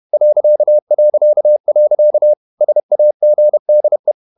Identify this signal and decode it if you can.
Morse code, heard as a keyed tone, about 31 wpm, 600 Hz. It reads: ... SAGDE
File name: Ferie QRQ 31wpm